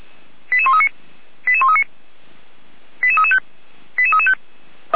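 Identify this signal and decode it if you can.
Five-tone selective call (ZVEI), heard as quick runs of five tones, tones of about 70 ms. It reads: 88139 88139 88335 88335
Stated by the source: ZVEI five-tone selective call received by SDR, from the feuerwehr-orlamuende fire brigade audio server.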